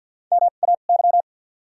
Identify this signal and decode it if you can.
Morse code, heard as a keyed tone, about 52 wpm, 700 Hz. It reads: M A Y
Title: 2x Character Spacing - Top 200 Words - 50wpm